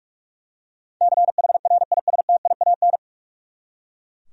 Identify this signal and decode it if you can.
Morse code, heard as a keyed tone, about 43 wpm, 700 Hz. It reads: CHRISTIAN
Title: USA Common Names - 45wpm